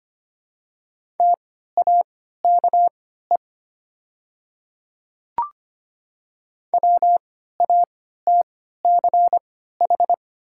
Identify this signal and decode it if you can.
Morse code, heard as a keyed tone, about 25 wpm, spaced as for 14 wpm, 700 Hz. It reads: TAKE WATCH
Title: 3x Character Spacing - Top 500 Words - 25wpm